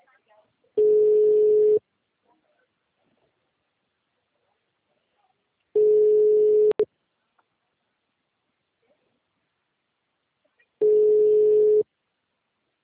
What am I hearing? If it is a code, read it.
Ringback tone (Europe, 425 Hz), with speech.